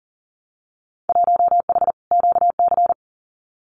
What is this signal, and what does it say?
Morse code, heard as a keyed tone, about 40 wpm, 700 Hz. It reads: 1H QC
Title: Winter Field Day Exchanges 40wpm